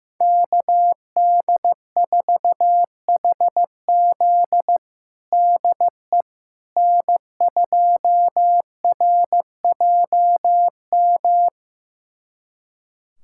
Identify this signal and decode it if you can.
Morse code, heard as a keyed tone, about 15 wpm, 700 Hz. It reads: KD4HZ DE N2RJM
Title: USA Callsign DE Callsign - 5 Characters - 15wpm